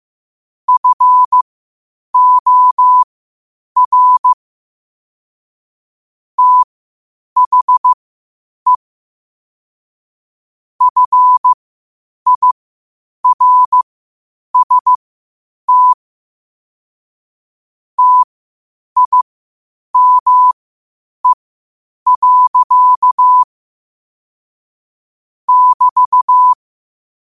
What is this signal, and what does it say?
Morse code, heard as a keyed tone, about 15 wpm, spaced as for 8 wpm, 1000 Hz. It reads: FOR THE FIRST TIME. <BT>
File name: qotdfiles / 8wpm